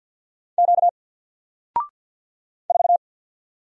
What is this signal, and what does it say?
Morse code, heard as a keyed tone, about 48 wpm, 700 Hz. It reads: Q 4
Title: Single Letter-Number Alt 50wpm